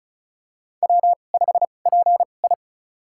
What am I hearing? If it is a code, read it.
Morse code, heard as a keyed tone, about 36 wpm, spaced as for 25 wpm, 700 Hz. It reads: W5PI
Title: USA Call Signs - 4 Characters - 2x Character Spacing - 35wpm